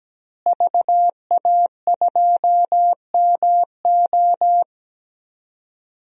Morse code, heard as a keyed tone, about 17 wpm, 700 Hz. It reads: VA2MO
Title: DXCC-100 Call Signs - 5 Characters - Repeated 3x - 17wpm